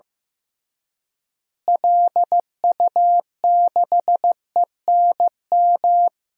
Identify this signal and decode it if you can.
Morse code, heard as a keyed tone, about 15 wpm, 700 Hz. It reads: LU6ENM
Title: DXCC-100 Call Signs - 6 Characters - Repeated 3x - 15wpm